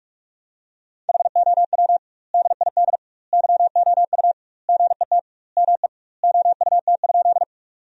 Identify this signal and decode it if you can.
Morse code, heard as a keyed tone, about 46 wpm, 700 Hz. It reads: HOW DID YOU GET ME OUT?
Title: 2x Word-Spacing - Sentences from Top 200 Words - 45wpm